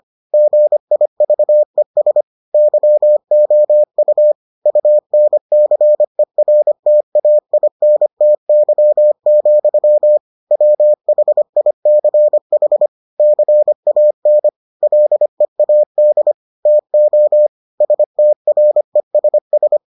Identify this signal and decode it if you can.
Morse code, heard as a keyed tone, about 25 wpm, 600 Hz. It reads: GIVES YOU UNCERTAINTY, WHICH CAN LEAD TO STRESS